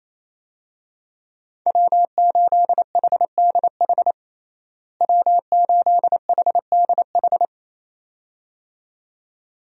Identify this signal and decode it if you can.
Morse code, heard as a keyed tone, about 28 wpm, 700 Hz. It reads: W8HDH W8HDH
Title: USA Call Signs - 5 Characters - Repeated 2x - 28wpm